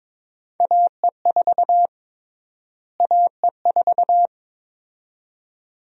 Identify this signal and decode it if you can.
Morse code, heard as a keyed tone, about 22 wpm, 700 Hz. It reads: AE4 AE4
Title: USA Call Sign Prefixes - Repeated 2x - 22wpm